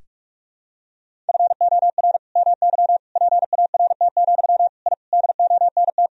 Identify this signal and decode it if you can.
Morse code, heard as a keyed tone, about 46 wpm, 700 Hz. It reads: FOR MY PART, I DONT
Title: Sentences from Top 1000 Words - Review Part 1 - 45wpm